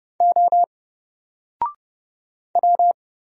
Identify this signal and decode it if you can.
Morse code, heard as a keyed tone, about 31 wpm, 700 Hz. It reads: O W
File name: Single-Letter-Number-30wpm